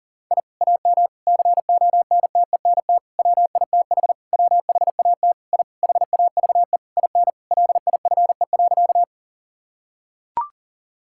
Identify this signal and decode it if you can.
Morse code, heard as a keyed tone, about 37 wpm, 700 Hz. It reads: I AM CONTENT WITH WHAT I HAVE IN LIFE.
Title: Sentences from Top 1000 Words - Review Part 6 - 40wpm